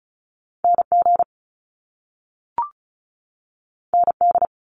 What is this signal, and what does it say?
Morse code, heard as a keyed tone, about 36 wpm, 700 Hz. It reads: NG ND
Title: Top Two Letter Combinations in Words 35wpm